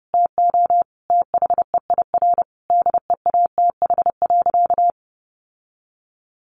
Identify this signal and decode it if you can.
Morse code, heard as a keyed tone, about 30 wpm, 700 Hz. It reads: TO THEIR DEATH.
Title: Sentences from Top 1100 Words 30wpm